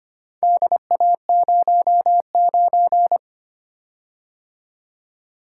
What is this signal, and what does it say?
Morse code, heard as a keyed tone, about 25 wpm, 700 Hz. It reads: DA09
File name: practice / 25wpm / Four Letter-Number Combinations - 25wpm